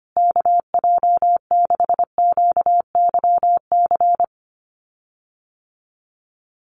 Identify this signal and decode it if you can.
Morse code, heard as a keyed tone, about 25 wpm, 700 Hz. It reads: KJ6QYC
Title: USA Call Signs Repeated 2x 25wpm